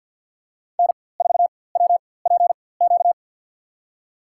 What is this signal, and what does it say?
Morse code, heard as a keyed tone, about 53 wpm, spaced as for 24 wpm, 690 Hz. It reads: N4WPQ
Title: USA Call Signs - 4x Character Spacing - 50wpm